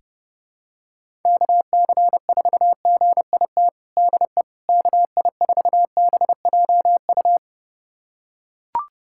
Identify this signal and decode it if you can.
Morse code, heard as a keyed tone, about 32 wpm, 700 Hz. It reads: KC4GIT DE KI4BJU
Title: USA Callsign DE Callsign 30wpm